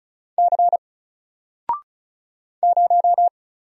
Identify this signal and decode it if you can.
Morse code, heard as a keyed tone, about 36 wpm, 700 Hz. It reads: C 0
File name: Single-Letter-Number-35wpm